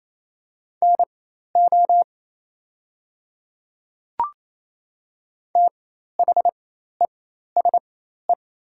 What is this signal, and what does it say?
Morse code, heard as a keyed tone, about 28 wpm, spaced as for 13 wpm, 700 Hz. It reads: NO THESE
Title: 4x Character Spacing - Top 100 Words - 28wpm